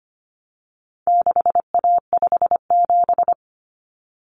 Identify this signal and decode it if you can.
Morse code, heard as a keyed tone, about 25 wpm, 700 Hz. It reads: 6A57
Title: Common Cut Numbers - 4-Digit 25wpm